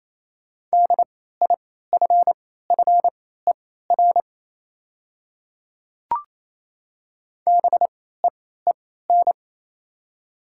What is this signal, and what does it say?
Morse code, heard as a keyed tone, about 28 wpm, spaced as for 16 wpm, 700 Hz. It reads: DIFFER BEEN
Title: 3x Character Spacing - Top 200 Words - 28wpm